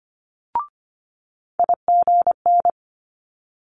Morse code, heard as a keyed tone, about 25 wpm, 700 Hz. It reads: IGN
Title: Multi-letter Phonemes Quick Fire 25wpm